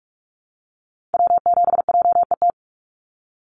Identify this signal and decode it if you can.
Morse code, heard as a keyed tone, about 44 wpm, 700 Hz. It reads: W7JET
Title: NA SOTA Activator Callsigns 45wpm